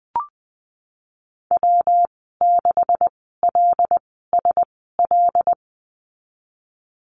Morse code, heard as a keyed tone, about 20 wpm, spaced as for 15 wpm, 700 Hz. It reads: W6LSL
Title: 245 Lesson 39g - Letter z - Call Signs 20wpm